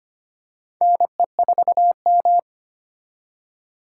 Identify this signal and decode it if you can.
Morse code, heard as a keyed tone, about 25 wpm, 700 Hz. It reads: NE4M
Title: K1USN SST QSO Practice - 25wpm